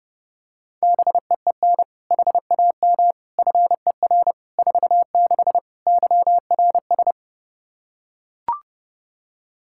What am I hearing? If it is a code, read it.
Morse code, heard as a keyed tone, about 30 wpm, 700 Hz. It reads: BEEN HAM FER 46 YRS
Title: K1USN SST QSO Practice 30wpm